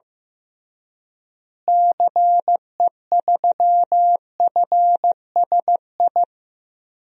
Morse code, heard as a keyed tone, about 15 wpm, 700 Hz. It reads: CE3FSI
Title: DXCC-100 Call Signs - 6 Characters - Repeated 2x - 15wpm